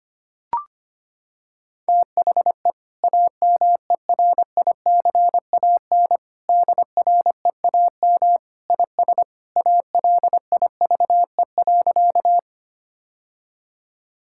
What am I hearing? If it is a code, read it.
Morse code, heard as a keyed tone, about 25 wpm, 700 Hz. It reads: THE AMERICAN DREAM IS ALIVE.
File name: practice / 25wpm / Sentences with 2-Word Phrases - 25wpm